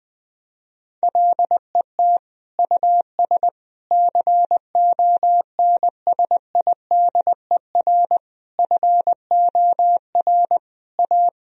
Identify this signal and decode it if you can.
Morse code, heard as a keyed tone, about 19 wpm, 700 Hz. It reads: LET US CONSIDER FOR A
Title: Sentences from Top 1000 Words - Review Part 4 - 20wpm